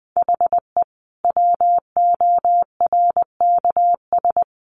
Morse code, heard as a keyed tone, about 19 wpm, 700 Hz. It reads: HE WORKS
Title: Sentences from Top 1100 Words 20wpm